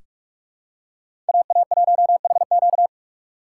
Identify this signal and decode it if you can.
Morse code, heard as a keyed tone, about 47 wpm, 700 Hz. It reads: AA1HQ
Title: USA Call Signs - 2x Character Spacing - 45wpm